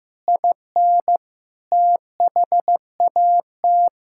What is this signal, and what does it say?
Morse code, heard as a keyed tone, about 16 wpm, 700 Hz. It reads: IN THAT
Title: Sentences from Top 1000 Words - Review Part 5 - 15wpm